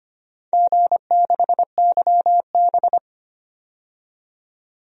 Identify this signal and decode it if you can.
Morse code, heard as a keyed tone, about 25 wpm, 700 Hz. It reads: G6YB
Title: DXCC-100 Call Signs - 4 Characters - 25wpm